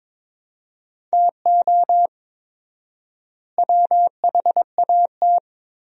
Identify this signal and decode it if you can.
Morse code, heard as a keyed tone, about 23 wpm, 700 Hz. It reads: TO WHAT
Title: 4x Word-Spacing - Sentences from Top 200 Words - 22wpm